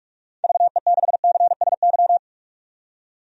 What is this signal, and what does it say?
Morse code, heard as a keyed tone, about 46 wpm, 700 Hz. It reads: VE6CSY
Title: DXCC-100 Call Signs - 6 Characters - 45wpm